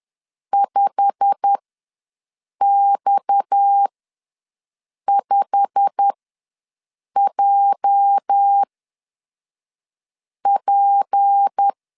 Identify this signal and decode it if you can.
Morse code, heard as a keyed tone, about 10 wpm, spaced as for 5 wpm, 800 Hz. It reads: 5X5JP
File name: LEZIONI / 8 YJ45 / 8 YJ45 ws 10wpm 3843